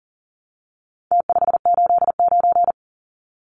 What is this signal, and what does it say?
Morse code, heard as a keyed tone, about 40 wpm, 700 Hz. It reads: T589
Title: Numbers - Contest Style - 4-Digit 40wpm